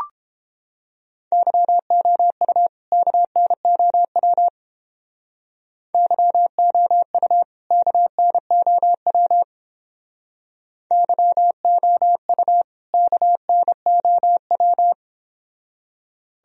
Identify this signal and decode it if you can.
Morse code, heard as a keyed tone, about 28 wpm, 700 Hz. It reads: YOU KNOW YOU KNOW YOU KNOW